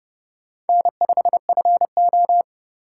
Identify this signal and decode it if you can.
Morse code, heard as a keyed tone, about 30 wpm, 700 Hz. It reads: N5FO
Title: USA Call Signs - 4 Characters - 2x Character Spacing - 30wpm